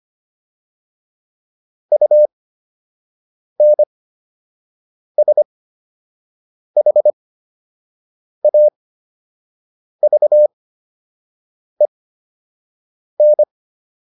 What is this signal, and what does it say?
Morse code, heard as a keyed tone, about 25 wpm, spaced as for 6 wpm, 600 Hz. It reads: UNSHAVEN